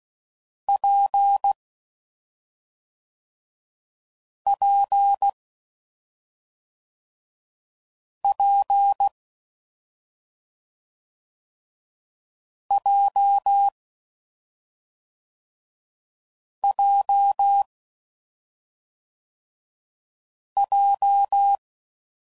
Morse code, heard as a keyed tone, about 16 wpm, 800 Hz. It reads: P P P J J J
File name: Combinations / Difficult Combos / WPJ WPJ1